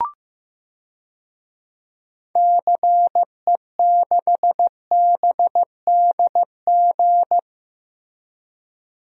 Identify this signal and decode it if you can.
Morse code, heard as a keyed tone, about 16 wpm, 700 Hz. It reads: CE6BDG
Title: DXCC-100 Call Signs - 6 Characters - Repeated 3x - 15wpm